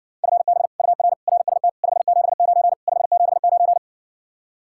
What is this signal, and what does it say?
Morse code, heard as a keyed tone, about 60 wpm, 700 Hz. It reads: FB UR RST 579 579